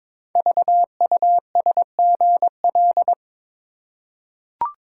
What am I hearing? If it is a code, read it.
Morse code, heard as a keyed tone, about 22 wpm, 700 Hz. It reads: VUSGL
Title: Code Groups - 5 Letters - 22wpm